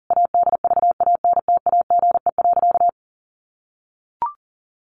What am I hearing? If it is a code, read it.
Morse code, heard as a keyed tone, about 40 wpm, 700 Hz. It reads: ADVANTAGE.
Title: Sentences from Top 1400 Words 40wpm